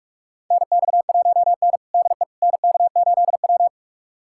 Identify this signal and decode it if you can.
Morse code, heard as a keyed tone, about 48 wpm, 700 Hz. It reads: NX1N DE NK8W
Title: USA Callsign DE Callsign - 4 Characters - 45wpm